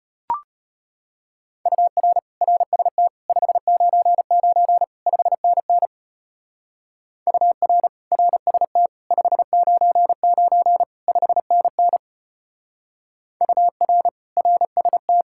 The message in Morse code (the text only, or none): UR RST 599 5NN UR RST 599 5NN UR RST